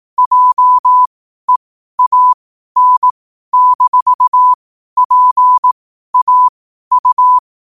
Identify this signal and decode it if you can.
Morse code, heard as a keyed tone, about 18 wpm, spaced as for 12 wpm, 1000 Hz. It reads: JEAN-PAU
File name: qotdfiles / 12wpm